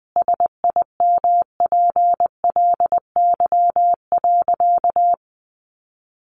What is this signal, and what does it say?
Morse code, heard as a keyed tone, about 20 wpm, 700 Hz. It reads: SIMPLY.
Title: Sentences from Top 600 Words 20wpm